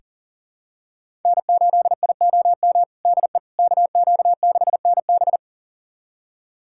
Morse code, heard as a keyed tone, about 37 wpm, 700 Hz. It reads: N8IOM DE KQ6NB
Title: USA Callsign DE Callsign - 5 Characters - 40wpm